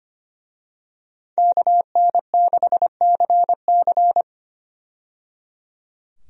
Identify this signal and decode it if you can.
Morse code, heard as a keyed tone, about 25 wpm, 700 Hz. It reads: KN6CC
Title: USA Call Signs - 5 Characters - 25wpm